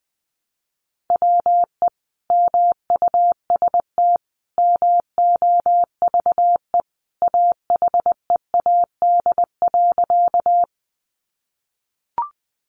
Morse code, heard as a keyed tone, about 20 wpm, 700 Hz. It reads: WE MUST MOVE AHEAD.
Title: Sentences from Top 1100 Words 20wpm